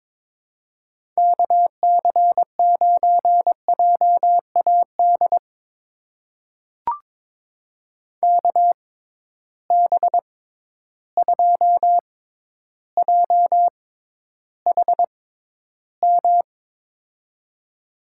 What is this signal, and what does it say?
Morse code, heard as a keyed tone, about 22 wpm, spaced as for 11 wpm, 700 Hz. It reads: KC9JAD KB2JHM
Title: USA Call Signs - 6 Characters - 6x Character Spacing - 22wpm